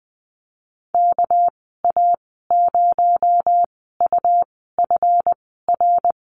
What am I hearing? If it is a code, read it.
Morse code, heard as a keyed tone, about 20 wpm, spaced as for 15 wpm, 700 Hz. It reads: KA0UFR
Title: 231 Lesson 37g - Letter x - Call Signs 20wpm